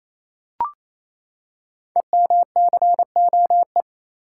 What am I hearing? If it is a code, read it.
Morse code, heard as a keyed tone, about 28 wpm, 700 Hz. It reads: EMCOE
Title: Code Groups - 5 Letter:Numbers - 28wpm